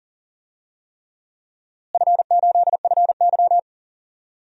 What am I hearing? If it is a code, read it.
Morse code, heard as a keyed tone, about 40 wpm, 700 Hz. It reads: F8FY
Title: DXCC-100 Call Signs - 4 Characters - Repeated 3x - 40wpm